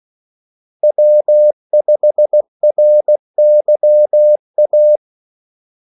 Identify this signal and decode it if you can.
Morse code, heard as a keyed tone, about 16 wpm, 600 Hz. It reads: W5RYA
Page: Kaldesignaler 16 wpm